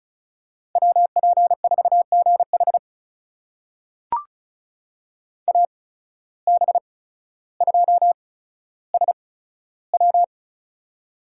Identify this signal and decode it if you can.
Morse code, heard as a keyed tone, about 36 wpm, spaced as for 15 wpm, 700 Hz. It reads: WP4GH AB2SW
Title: USA Call Signs - 5 Characters - 8x Character Spacing - 35wpm